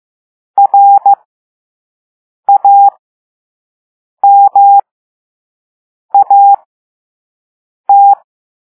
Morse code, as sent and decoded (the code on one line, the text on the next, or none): .-. .- -- .- -
RAMAT